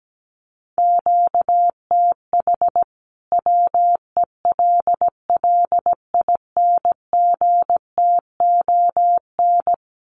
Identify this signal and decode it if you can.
Morse code, heard as a keyed tone, about 17 wpm, 700 Hz. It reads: QTH WELLINGTON
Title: Common QSO Phrases 17wpm